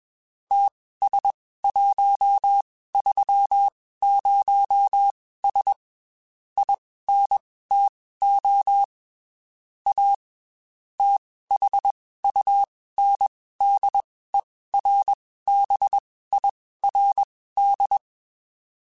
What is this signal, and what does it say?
Morse code, heard as a keyed tone, about 21 wpm, spaced as for 15 wpm, 800 Hz. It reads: TS130S INTO A THUNDERBIRD
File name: QSOs / QSOs-mp3s / WD0BGP 1@20